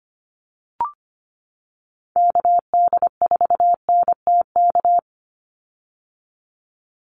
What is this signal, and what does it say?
Morse code, heard as a keyed tone, about 25 wpm, 700 Hz. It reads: KD4NTK
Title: USA Call Signs 25wpm